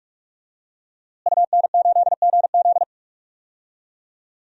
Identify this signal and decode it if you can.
Morse code, heard as a keyed tone, about 43 wpm, 700 Hz. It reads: UN8GZ
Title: DXCC-100 Call Signs - 5 Characters - Repeated 3x - 45wpm